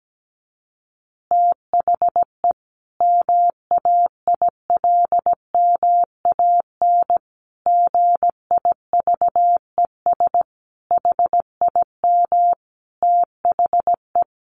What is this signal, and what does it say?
Morse code, heard as a keyed tone, about 17 wpm, 700 Hz. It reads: THE MAILMAN GIVES HIM THE
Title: Easy Phrases 17wpm